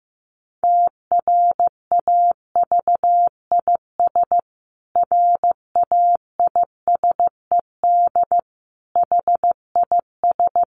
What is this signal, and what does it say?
Morse code, heard as a keyed tone, about 15 wpm, 700 Hz. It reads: TRAVIS RAISED HIS
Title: A Sound Of Thunder 15wpm